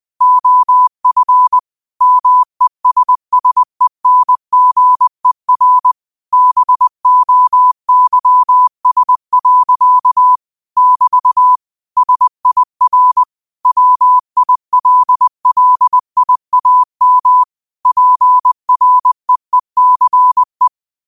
Morse code, heard as a keyed tone, about 20 wpm, 1000 Hz. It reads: OF MESSENGER BOYS. <BT> SIR WILLIAM PREECE
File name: qotdfiles / 20wpm